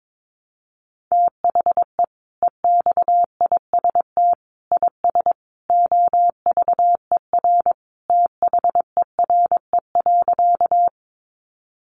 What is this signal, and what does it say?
Morse code, heard as a keyed tone, about 22 wpm, 700 Hz. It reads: THE EXIST IS OVER THERE.
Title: Sentences from Top 1100 Words 22wpm